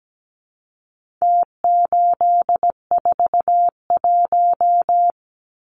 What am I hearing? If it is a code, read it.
Morse code, heard as a keyed tone, about 17 wpm, 700 Hz. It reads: T841
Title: Numbers - Contest Style - 4-Digit 17wpm